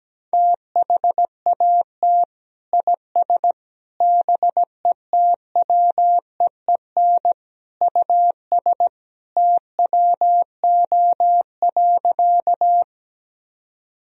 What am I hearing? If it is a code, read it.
Morse code, heard as a keyed tone, about 17 wpm, 700 Hz. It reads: THAT IS BETWEEN US TWO.
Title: Sentences from Top 1000 Words - Review Part 1 - 17wpm